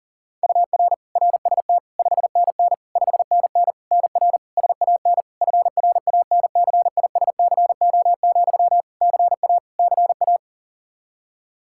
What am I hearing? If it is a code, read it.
Morse code, heard as a keyed tone, about 40 wpm, 700 Hz. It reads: UR RST 5NN 5NN NR SAN FRANCISCO, CA CA
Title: Practice QSO - Exchange 1 - Beginning - 40wpm